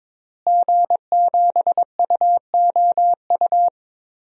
Morse code, heard as a keyed tone, about 22 wpm, 700 Hz. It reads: G7UOU
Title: DXCC-100 Call Signs - 5 Characters - Repeated 3x - 22wpm